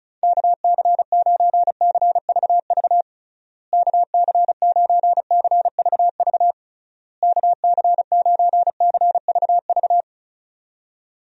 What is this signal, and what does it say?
Morse code, heard as a keyed tone, about 34 wpm, 700 Hz. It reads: KC9CVV KC9CVV KC9CVV
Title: USA Call Signs - 6 Characters - Repeated 3x - 35wpm